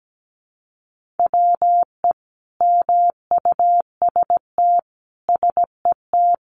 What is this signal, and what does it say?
Morse code, heard as a keyed tone, about 17 wpm, 700 Hz. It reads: WE MUST SET
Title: Sentences from Top 1400 Words 17wpm